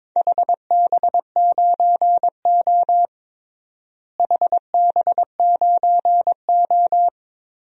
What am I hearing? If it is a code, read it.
Morse code, heard as a keyed tone, about 22 wpm, 700 Hz. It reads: HB9O HB9O
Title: DXCC-100 Call Signs - 4 Characters - Repeated 2x - 22wpm